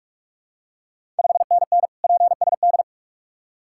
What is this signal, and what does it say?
Morse code, heard as a keyed tone, about 44 wpm, 690 Hz. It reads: 5NN PSD
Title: Ontario QSO Party Exchange 45wpm